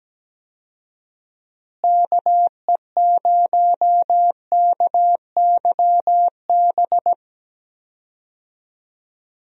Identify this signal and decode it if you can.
Morse code, heard as a keyed tone, about 17 wpm, 700 Hz. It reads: KE0KYB
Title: USA Call Signs - 6 Characters - Repeated 3x - 17wpm